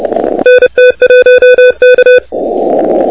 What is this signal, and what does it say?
Morse code, heard as a keyed tone, about 29 wpm, 1500 Hz, with noise. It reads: NT1K
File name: recordings / W1WEF de NT1K ARRL10M CW 28MHz